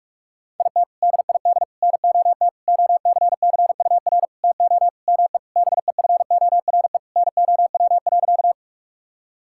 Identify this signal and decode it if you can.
Morse code, heard as a keyed tone, about 46 wpm, 700 Hz. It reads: IT DID NOT OCCUR TO ME BEFORE NOW.
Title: Sentences from Top 1000 Words - Review Part 6 - 45wpm